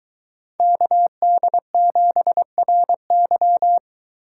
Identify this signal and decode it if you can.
Morse code, heard as a keyed tone, about 23 wpm, 700 Hz. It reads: KD7RY